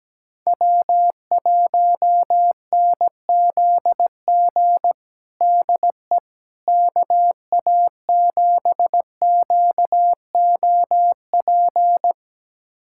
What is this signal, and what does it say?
Morse code, heard as a keyed tone, about 17 wpm, 700 Hz. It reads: W1NZG DE KA7QOP
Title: USA Callsign DE Callsign 17wpm